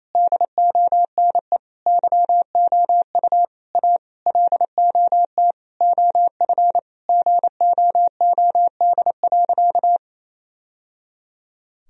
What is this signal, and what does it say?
Morse code, heard as a keyed tone, about 28 wpm, 700 Hz. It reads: DONE YOU A LOT OF GOOD.
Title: Sentences from Top 1000 Words - Review Part 3 - 28wpm